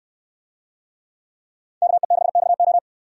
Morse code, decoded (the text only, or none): KE6XO